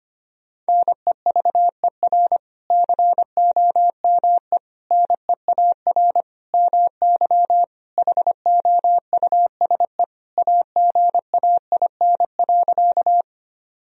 Morse code, heard as a keyed tone, about 25 wpm, 700 Hz. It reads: NEVER COME NEAR MY HOUSE AGAIN.
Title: Sentences from Top 1000 Words - Review Part 1 - 25wpm